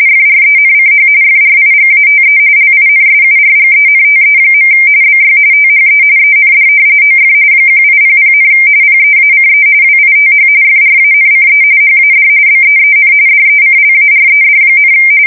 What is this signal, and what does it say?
RTTY, about 45.45 baud. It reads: FRENCH FRIGATE LINSURGENTE. BY THE END OF 1800, PEACE WITH FRANCE HAD BEEN DECLARED,
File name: fsk170 / usnavy1